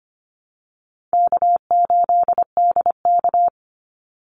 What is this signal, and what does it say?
Morse code, heard as a keyed tone, about 25 wpm, 700 Hz. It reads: K8DK
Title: Easy Call Signs - 25wpm